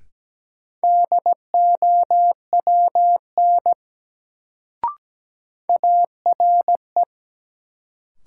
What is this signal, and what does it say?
Morse code, heard as a keyed tone, about 17 wpm, 700 Hz. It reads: DOWN ARE
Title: Sets of 2 Words - New Word Drill - 17wpm